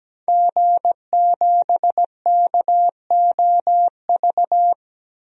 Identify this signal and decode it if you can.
Morse code, heard as a keyed tone, about 17 wpm, 700 Hz. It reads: G7KOV
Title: DXCC-100 Call Signs - 5 Characters - 17wpm